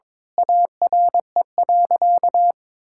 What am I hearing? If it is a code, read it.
Morse code, heard as a keyed tone, about 22 wpm, 700 Hz. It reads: ARE.
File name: practice / 22wpm / Sentences with 2-Word Phrases - 22wpm